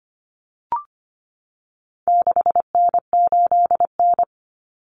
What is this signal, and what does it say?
Morse code, heard as a keyed tone, about 25 wpm, 700 Hz. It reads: 6N8N
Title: Common Cut Numbers - 4-Digit 25wpm